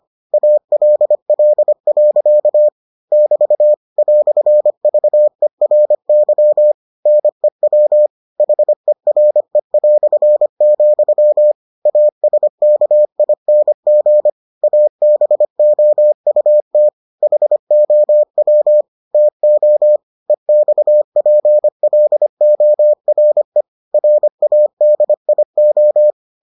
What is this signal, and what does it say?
Morse code, heard as a keyed tone, about 25 wpm, 600 Hz. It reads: ALL. <BT> "VERY NEW HERE", ASKING ABOUT HOW TO EXPLORE RADIO